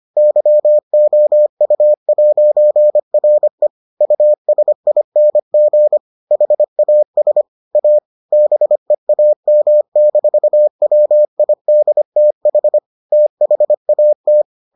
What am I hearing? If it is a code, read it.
Morse code, heard as a keyed tone, about 25 wpm, 600 Hz. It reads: YOU'RE USING HAS A BEAM-WIDTH THAT